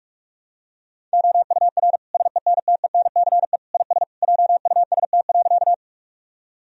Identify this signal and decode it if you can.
Morse code, heard as a keyed tone, about 46 wpm, 700 Hz. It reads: OUR SENTENCE IS JUST.
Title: Sentences from Top 1000 Words - Review Part 1 - 45wpm